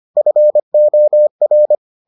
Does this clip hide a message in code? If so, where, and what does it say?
Yes: Morse code from 0.2 to 1.8 s, heard as a keyed tone, about 25 wpm, 600 Hz. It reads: FOR